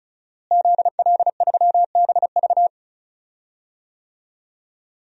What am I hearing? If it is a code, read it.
Morse code, heard as a keyed tone, about 36 wpm, 690 Hz. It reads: ZL3BV
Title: DXCC-100 Call Signs - 5 Characters - Repeated 3x - 35wpm